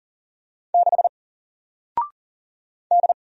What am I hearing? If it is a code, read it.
Morse code, heard as a keyed tone, about 39 wpm, 700 Hz. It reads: 6 D
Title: Single Letter-Number 40wpm